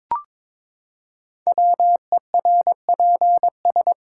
Morse code, heard as a keyed tone, about 22 wpm, 700 Hz. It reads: WERPS
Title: Code Groups - 5 Letters - 22wpm